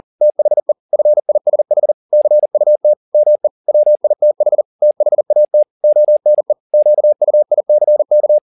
Morse code, heard as a keyed tone, about 40 wpm, 600 Hz. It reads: THE FISH CUT ME WITH THAT ONE QUICK